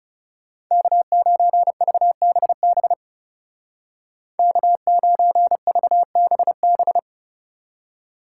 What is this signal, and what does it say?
Morse code, heard as a keyed tone, about 40 wpm, 700 Hz. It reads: K9VBB K9VBB